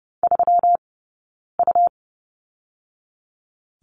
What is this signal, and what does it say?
Morse code, heard as a keyed tone, about 30 wpm, spaced as for 9 wpm, 700 Hz. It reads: 3U
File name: Lesson 20d - Number 3 - Two Characters 30wpm